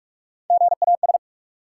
Morse code, heard as a keyed tone, about 42 wpm, 700 Hz. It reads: GAS
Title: Sentences from Top 1100 Words - New Word Drill - 45wpm